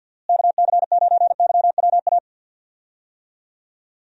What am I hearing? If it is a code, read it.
Morse code, heard as a keyed tone, about 50 wpm, 700 Hz. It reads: KC9YWA
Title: USA Call Signs - 6 Characters - Repeated 2x - 50wpm